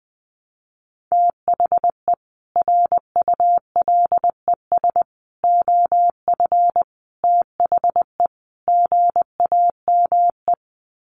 Morse code, heard as a keyed tone, about 20 wpm, 700 Hz. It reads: THE RULES OF THE GAME
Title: Sets of 5 Words Encore2 20wpm